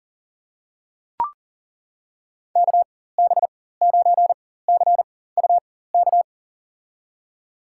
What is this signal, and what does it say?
Morse code, heard as a keyed tone, about 41 wpm, spaced as for 19 wpm, 700 Hz. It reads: KB9CUK
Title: USA Call Signs - 6 Characters - 4x Character Spacing - 40wpm